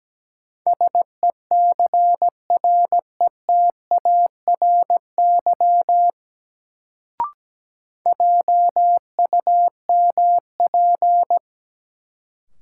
Morse code, heard as a keyed tone, about 17 wpm, 700 Hz. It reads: SECRETARY JUMP
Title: Sentences from Top 1300 Words - New Word Drill Alt - 17wpm